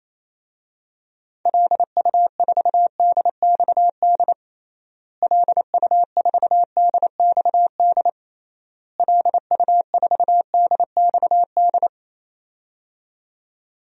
Morse code, heard as a keyed tone, about 28 wpm, 700 Hz. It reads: LU4DXD LU4DXD LU4DXD
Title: DXCC-100 Call Signs - 6 Characters - Repeated 3x - 28wpm